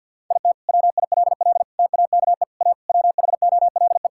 Morse code, heard as a keyed tone, about 50 wpm, 700 Hz. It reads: IT WILL TAKE A WHOLE
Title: Sentences from Top 1000 Words - Review Part 5 - 50wpm